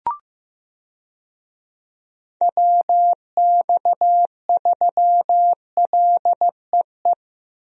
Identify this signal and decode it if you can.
Morse code, heard as a keyed tone, about 15 wpm, 700 Hz. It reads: WX3LEE
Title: USA Call Signs - 6 Characters - Repeated 2x - 15wpm